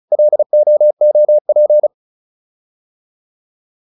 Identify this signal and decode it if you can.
Morse code, heard as a keyed tone, about 34 wpm, 600 Hz. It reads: LOOP